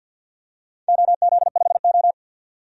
Morse code, heard as a keyed tone, about 47 wpm, 700 Hz. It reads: OZ5O
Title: DXCC-100 Call Signs - 4 Characters - 50wpm